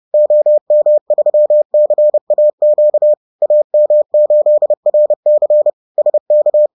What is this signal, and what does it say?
Morse code, heard as a keyed tone, about 30 wpm, 600 Hz. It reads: OM3CAQ AM8RC SK